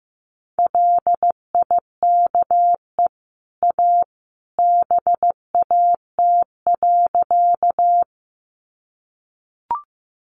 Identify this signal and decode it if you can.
Morse code, heard as a keyed tone, about 15 wpm, 700 Hz. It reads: LIKE A BAT.
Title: Sentences from Top 1500 Words 15wpm